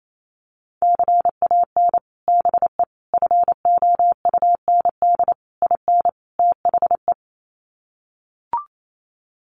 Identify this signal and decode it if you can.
Morse code, heard as a keyed tone, about 28 wpm, 700 Hz. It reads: CAN BE FOUND IN THE
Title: Sets of 5 Words Encore 28wpm